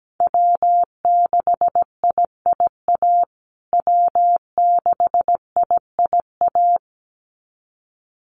Morse code, heard as a keyed tone, about 17 wpm, 700 Hz. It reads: W6IIA W6IIA
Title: USA Call Signs Repeated 2x 17wpm